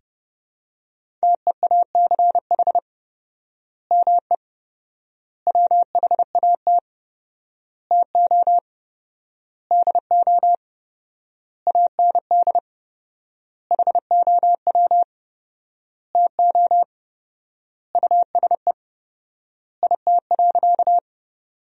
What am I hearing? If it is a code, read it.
Morse code, heard as a keyed tone, about 32 wpm, 700 Hz. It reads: TEACH ME WHAT TO DO AND HOW TO USE IT.
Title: 4x Word-Spacing - Sentences from Top 500 Words - 30wpm